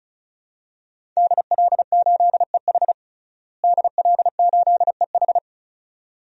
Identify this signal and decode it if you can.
Morse code, heard as a keyed tone, about 34 wpm, 700 Hz. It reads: DL8EH DL8EH
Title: DXCC-100 Call Signs - 5 Characters - Repeated 2x - 35wpm